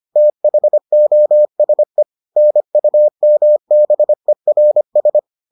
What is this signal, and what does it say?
Morse code, heard as a keyed tone, about 25 wpm, 600 Hz. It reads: THOSE NUMBERS